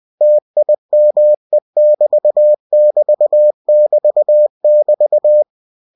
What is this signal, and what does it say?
Morse code, heard as a keyed tone, about 20 wpm, 600 Hz. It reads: TIME<BT><BT><BT><BT>
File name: cw-books / winnie-the-pooh / winnie-20wpm-page-0025